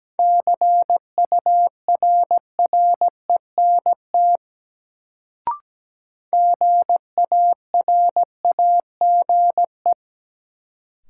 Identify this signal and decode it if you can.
Morse code, heard as a keyed tone, about 17 wpm, 700 Hz. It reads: CURRENT GARAGE
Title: Sentences from Top 1600 Words - New Word Drill Alt - 17wpm